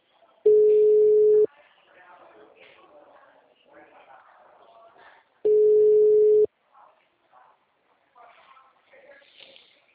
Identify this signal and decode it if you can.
Ringback tone (Europe, 425 Hz), with speech.